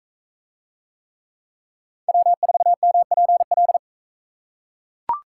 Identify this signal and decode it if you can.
Morse code, heard as a keyed tone, about 38 wpm, 700 Hz. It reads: W4MPL